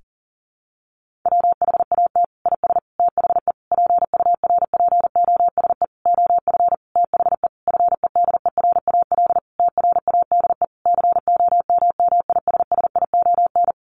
Morse code, heard as a keyed tone, about 37 wpm, 700 Hz. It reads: WHAT IS THE PURPOSE OF THE FEDERAL TRADE COMMISSION
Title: Sentences from Top 1400 Words 40wpm